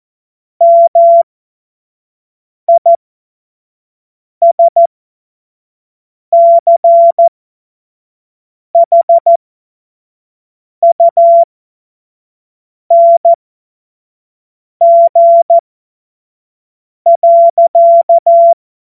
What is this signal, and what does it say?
Morse code, heard as a keyed tone, about 14 wpm, spaced as for 5 wpm, 680 Hz. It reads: MISCHUNG.
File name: Morsekurs / Witze / Witz 15. August 2023 25 F70